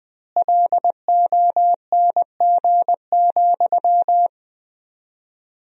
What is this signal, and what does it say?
Morse code, heard as a keyed tone, about 21 wpm, 700 Hz. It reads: LONG,
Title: 4x Word-Spacing - Sentences from Top 500 Words - 20wpm